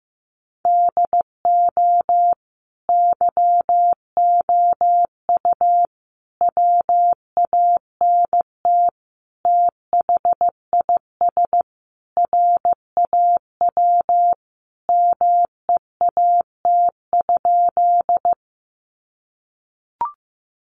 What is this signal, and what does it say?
Morse code, heard as a keyed tone, about 15 wpm, 700 Hz. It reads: DO YOU WANT THIS RAW MEAT?
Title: Sentences from Top 1500 Words 15wpm